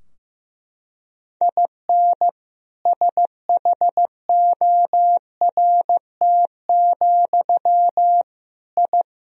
Morse code, heard as a keyed tone, about 15 wpm, 700 Hz. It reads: IN SHORT, I
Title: Sentences from Top 1000 Words - Review Part 2 - 15wpm